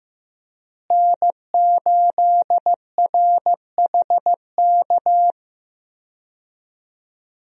Morse code, heard as a keyed tone, about 16 wpm, 700 Hz. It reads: N8RHK
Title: USA Call Signs - 5 Characters - Repeated 3x - 15wpm